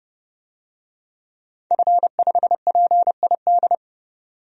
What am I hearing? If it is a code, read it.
Morse code, heard as a keyed tone, about 29 wpm, 700 Hz. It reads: F5PID
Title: DXCC-100 Call Signs - 5 Characters - 30wpm